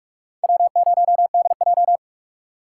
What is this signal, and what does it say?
Morse code, heard as a keyed tone, about 57 wpm, 700 Hz. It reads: W0DJ